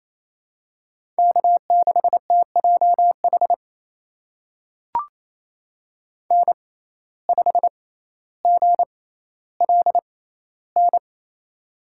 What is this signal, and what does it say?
Morse code, heard as a keyed tone, about 28 wpm, spaced as for 14 wpm, 700 Hz. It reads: K6TJH N5GLN
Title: USA Call Signs - 5 Characters - 6x Character Spacing - 28wpm